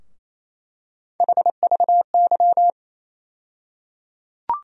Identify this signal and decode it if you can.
Morse code, heard as a keyed tone, about 28 wpm, 700 Hz. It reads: HVY
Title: USA Call Sign Postfixes - 3 Letters 28wpm